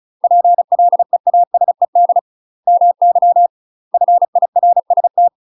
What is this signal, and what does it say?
Morse code, heard as a keyed tone, about 34 wpm, 700 Hz. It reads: PLEASED MY FIRST
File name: UK MASTER QSOs TRAINING / 35WPM / UK MASTER QSOs 0021